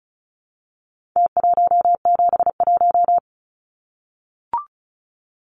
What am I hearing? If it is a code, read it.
Morse code, heard as a keyed tone, about 34 wpm, 700 Hz. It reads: T171
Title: Numbers - Contest Style - 4-Digit 35wpm